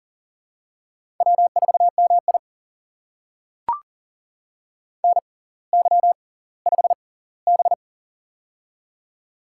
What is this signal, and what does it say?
Morse code, heard as a keyed tone, about 40 wpm, spaced as for 20 wpm, 700 Hz. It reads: W4MI NY5B
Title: USA Call Signs - 4 Characters - 6x Character Spacing - 40wpm